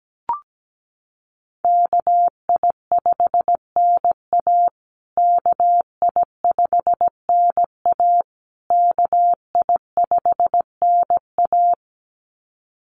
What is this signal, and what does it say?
Morse code, heard as a keyed tone, about 17 wpm, 700 Hz. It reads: KI5NA KI5NA KI5NA
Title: USA Call Signs Repeated 3x 17wpm